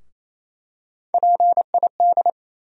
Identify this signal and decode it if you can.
Morse code, heard as a keyed tone, about 28 wpm, 700 Hz. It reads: PID
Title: USA Call Sign Postfixes - 3 Letters 28wpm